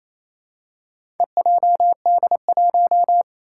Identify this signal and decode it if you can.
Morse code, heard as a keyed tone, about 28 wpm, 700 Hz. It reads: EJD1
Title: Four Challenging Letter-Number Combinations 28wpm